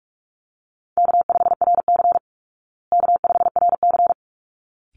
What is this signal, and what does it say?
Morse code, heard as a keyed tone, about 44 wpm, 700 Hz. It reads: K5RC K5RC
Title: Call Signs Repeated 3x - 45wpm